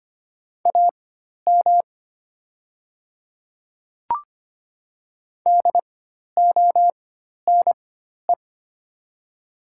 Morse code, heard as a keyed tone, about 25 wpm, spaced as for 12 wpm, 700 Hz. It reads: AM DONE
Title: 4x Character Spacing - Top 500 Words - 25wpm